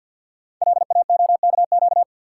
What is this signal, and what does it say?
Morse code, heard as a keyed tone, about 46 wpm, 700 Hz. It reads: LAOKQ
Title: Code Groups - 5 Letters - 50wpm